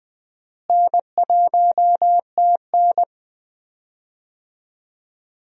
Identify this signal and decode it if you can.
Morse code, heard as a keyed tone, about 20 wpm, 700 Hz. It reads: N1TN
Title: USA Call Signs - 4 Characters - Repeated 2x - 20wpm